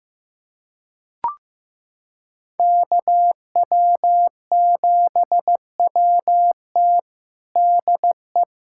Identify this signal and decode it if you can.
Morse code, heard as a keyed tone, about 15 wpm, 700 Hz. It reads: KW7WT DE
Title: USA Callsign DE Callsign - 5 Characters - 15wpm